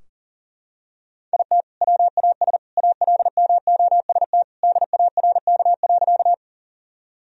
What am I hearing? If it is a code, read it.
Morse code, heard as a keyed tone, about 40 wpm, 700 Hz. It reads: IT WAS ALMOST DARK.
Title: Sentences from Top 1000 Words - Review Part 4 - 40wpm